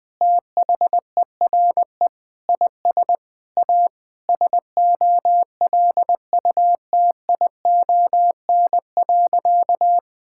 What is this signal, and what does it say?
Morse code, heard as a keyed tone, about 20 wpm, 700 Hz. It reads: THERE IS A SOLUTION.
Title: Sentences with 4-Word Phrases - Spoken Beginning - 20wpm